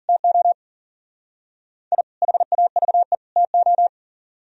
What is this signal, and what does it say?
Morse code, heard as a keyed tone, about 51 wpm, 700 Hz. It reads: TO I HAVE TO